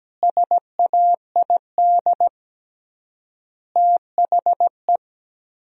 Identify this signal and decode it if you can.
Morse code, heard as a keyed tone, about 17 wpm, 700 Hz. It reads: SAID THE
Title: 3x Word-Spacing - Sentences from Top 200 Words - 17wpm